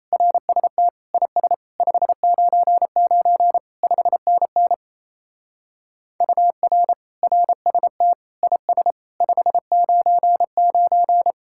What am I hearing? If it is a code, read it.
Morse code, heard as a keyed tone, about 34 wpm, 700 Hz. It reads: RST IS 599 5NN UR RST IS 599
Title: Speed-Racing Common QSO Phrases - 28wpm